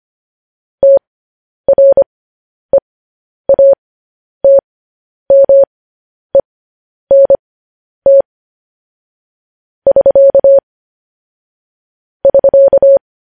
Morse code, heard as a keyed tone, about 25 wpm, spaced as for 10 wpm, 550 Hz. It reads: TREATMENT <SK> <SK>
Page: news-headlines-in-morse-code-at-10-wpm